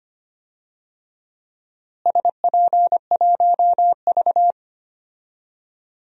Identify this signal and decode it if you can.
Morse code, heard as a keyed tone, about 25 wpm, 700 Hz. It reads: SP1V
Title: DXCC-100 Call Signs - 4 Characters - Repeated 3x - 25wpm